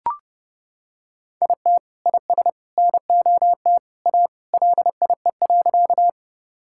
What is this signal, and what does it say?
Morse code, heard as a keyed tone, about 31 wpm, 700 Hz. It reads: IT IS NOT A LIE.
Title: Sentences from Top 1600 Words 30wpm